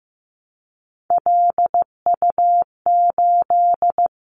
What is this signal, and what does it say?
Morse code, heard as a keyed tone, about 15 wpm, 700 Hz. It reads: LU8
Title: Callsign Prefixes for Top 100 DXCC - 15wpm